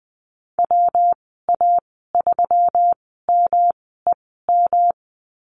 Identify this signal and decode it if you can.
Morse code, heard as a keyed tone, about 20 wpm, spaced as for 14 wpm, 700 Hz. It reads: WA3MEM
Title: 122 Lesson 20g - Number 3 - Call Signs 20wpm